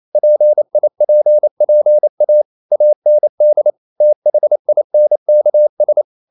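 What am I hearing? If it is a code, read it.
Morse code, heard as a keyed tone, about 28 wpm, 600 Hz. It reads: PIPPA AND THINKS
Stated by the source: Morse code practice at 28 wpm, text of Ken Follett's novel Never in English, from CW med Gnister.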